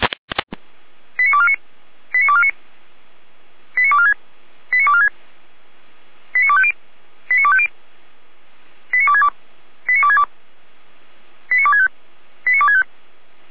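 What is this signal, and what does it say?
Five-tone selective call (ZVEI), heard as quick runs of five tones, tones of about 70 ms. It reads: 89249 89249 89246 89246 89250 89250 89262 89262 89265 89265
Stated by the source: ZVEI five-tone call received on an SDR.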